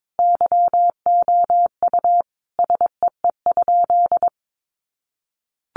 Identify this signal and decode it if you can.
Morse code, heard as a keyed tone, about 22 wpm, 700 Hz. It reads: YOU SEE?
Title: Sentences from Top 100 Words 22wpm